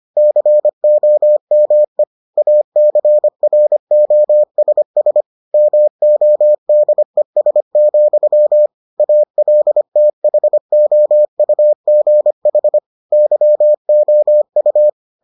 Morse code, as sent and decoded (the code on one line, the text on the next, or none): -.-. --- -- . / .- -.-. .-. --- ... ... / -- --- -.. ... --..-- / .- .-.. - .... --- ..- --. .... / -.-- --- ..-
COME ACROSS MODES, ALTHOUGH YOU